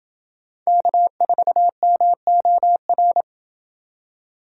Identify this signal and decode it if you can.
Morse code, heard as a keyed tone, about 29 wpm, 700 Hz. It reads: K4MOR